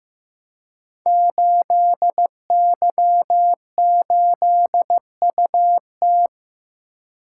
Morse code, heard as a keyed tone, about 15 wpm, 700 Hz. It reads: 8Y8UT
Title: Code Groups - 5 Letter:Numbers - 15wpm